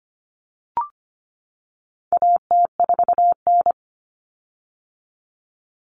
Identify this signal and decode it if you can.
Morse code, heard as a keyed tone, about 25 wpm, 700 Hz. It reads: AT4N
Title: Common Cut Numbers - 4-Digit 25wpm